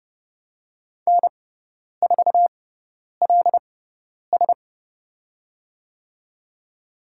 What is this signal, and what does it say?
Morse code, heard as a keyed tone, about 30 wpm, spaced as for 10 wpm, 700 Hz. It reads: N4LS
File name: Lesson 16g - Letter c - Call Signs 30wpm